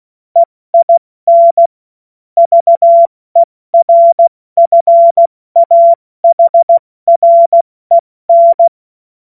Morse code, heard as a keyed tone, about 16 wpm, 680 Hz. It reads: EIN VERFAHREN
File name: Morsekurs / Witze / Witz 10.Oktober 2023 70 F80